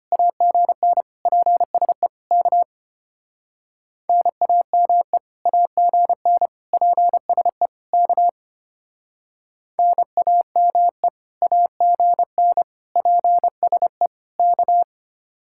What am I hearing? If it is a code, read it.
Morse code, heard as a keyed tone, about 33 wpm, 700 Hz. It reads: AGN PSE K NAME AGN PSE K NAME AGN PSE K